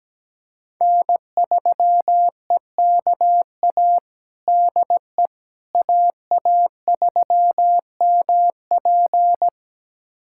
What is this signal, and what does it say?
Morse code, heard as a keyed tone, about 17 wpm, 700 Hz. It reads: N3EKA DE AA3MP
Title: USA Callsign DE Callsign - 5 Characters - 17wpm